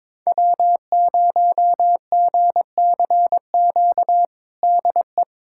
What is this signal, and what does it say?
Morse code, heard as a keyed tone, about 21 wpm, 700 Hz. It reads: W0GCQ DE
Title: USA Callsign DE Callsign 22wpm